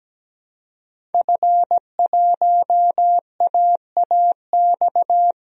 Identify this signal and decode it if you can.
Morse code, heard as a keyed tone, about 17 wpm, 700 Hz. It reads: F1AAX
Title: DXCC-100 Call Signs - 5 Characters - 17wpm